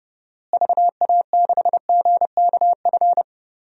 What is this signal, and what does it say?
Morse code, heard as a keyed tone, about 30 wpm, 700 Hz. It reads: VA6GKF
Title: DXCC-100 Call Signs - 6 Characters - Repeated 3x - 30wpm